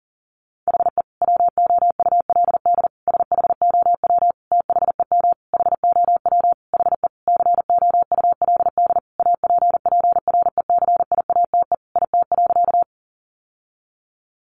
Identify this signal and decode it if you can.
Morse code, heard as a keyed tone, about 40 wpm, 700 Hz. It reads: HE WOULD SHOW THEM HOW HE COULD APPRECIATE IT.
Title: Sentences from Top 1300 Words 40wpm